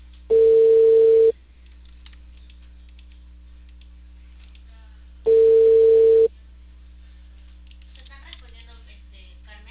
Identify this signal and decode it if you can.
Ringback tone (China, 450 Hz), with speech.